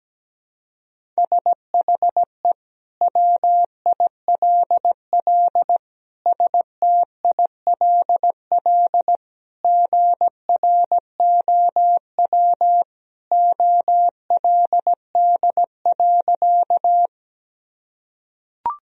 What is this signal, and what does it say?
Morse code, heard as a keyed tone, about 17 wpm, 700 Hz. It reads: SHE WILL STILL GROW OLD.
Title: Sentences from Top 1000 Words - Review Part 1 - 17wpm